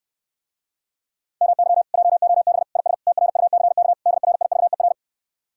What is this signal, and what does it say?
Morse code, heard as a keyed tone, about 69 wpm, 700 Hz. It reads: MY JOB IS TRUCK DRIVER